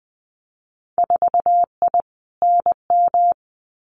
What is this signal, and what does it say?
Morse code, heard as a keyed tone, about 19 wpm, 700 Hz. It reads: 4I NM
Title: Winter Field Day Exchanges Encore 20wpm